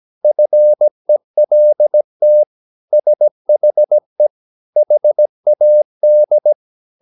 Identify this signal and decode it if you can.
Morse code, heard as a keyed tone, about 17 wpm, 600 Hz. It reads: FELT SHE HAD
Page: Romanen Never af Ken Follet på engelsk.